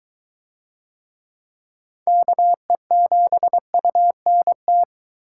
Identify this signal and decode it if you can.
Morse code, heard as a keyed tone, about 20 wpm, 700 Hz. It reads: KE7UNT